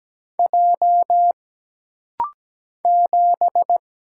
Morse code, heard as a keyed tone, about 17 wpm, 700 Hz. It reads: J 7
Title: Single Letter-Number - Rapid-Fire 17wpm